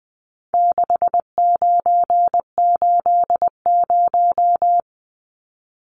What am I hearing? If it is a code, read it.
Morse code, heard as a keyed tone, about 20 wpm, 700 Hz. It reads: 6980
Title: Numbers - 4-Digit 20wpm